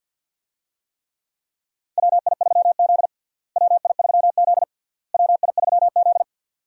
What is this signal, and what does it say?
Morse code, heard as a keyed tone, about 51 wpm, 700 Hz. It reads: WI3Z WI3Z WI3Z
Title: USA Call Signs - 4 Characters - Repeated 3x - 50wpm